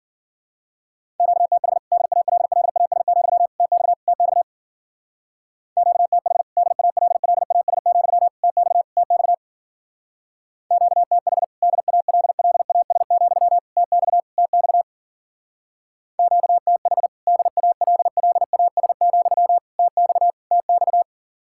Morse code, heard as a keyed tone, about 48 wpm, 700 Hz. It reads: QTH DALLAS, TX TX QTH DALLAS, TX TX QTH DALLAS, TX TX QTH DALLAS, TX TX
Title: Speed-Racing Common QSO Phrases - 40wpm